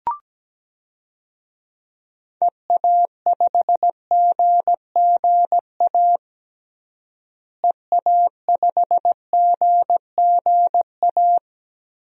Morse code, heard as a keyed tone, about 17 wpm, 700 Hz. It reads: EA5GGA EA5GGA
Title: DXCC-100 Call Signs - 6 Characters - Repeated 3x - 17wpm